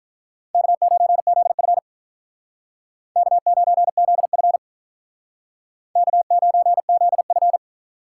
Morse code, heard as a keyed tone, about 46 wpm, 700 Hz. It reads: K9ZF K9ZF K9ZF